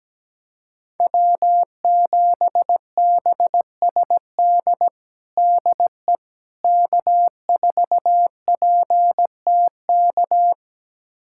Morse code, heard as a keyed tone, about 17 wpm, 700 Hz. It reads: W7BSD DE K4PTK
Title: USA Callsign DE Callsign - 5 Characters - 17wpm